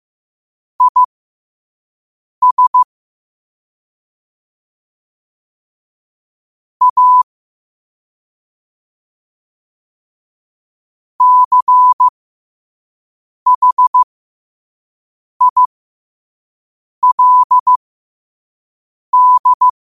Morse code, heard as a keyed tone, about 15 wpm, spaced as for 5 wpm, 1000 Hz. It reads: IS A CHILD
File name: qotdfiles / 5wpm